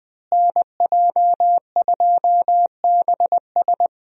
Morse code, heard as a keyed tone, about 20 wpm, 700 Hz. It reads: NJ2BS
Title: USA Call Signs - 5 Characters - Repeated 3x - 20wpm